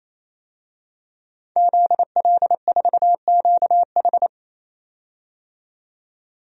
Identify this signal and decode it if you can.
Morse code, heard as a keyed tone, about 28 wpm, 700 Hz. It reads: ZL4QH
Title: DXCC-100 Call Signs - 5 Characters - 28wpm